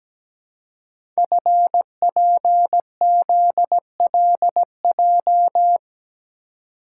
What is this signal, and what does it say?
Morse code, heard as a keyed tone, about 17 wpm, 700 Hz. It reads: FPZLJ
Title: Code Groups - 5 Letter:Numbers - 17wpm